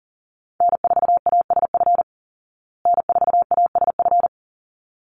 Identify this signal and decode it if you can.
Morse code, heard as a keyed tone, about 40 wpm, 700 Hz. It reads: N4ASF N4ASF
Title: Call Signs Repeated 3x - 40wpm